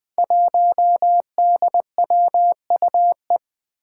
Morse code, heard as a keyed tone, about 20 wpm, 700 Hz. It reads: 1DWUE
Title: Five Challenging Letter-Number Combinations 20wpm